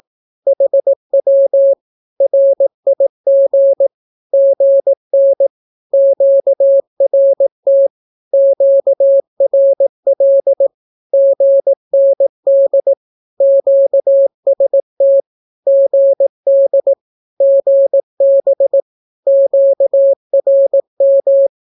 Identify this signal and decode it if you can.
Morse code, heard as a keyed tone, about 18 wpm, 550 Hz. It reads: HW RIG GN QRT QRL GND QST GD GB QRM
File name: naukapodstaw / skroty / skroty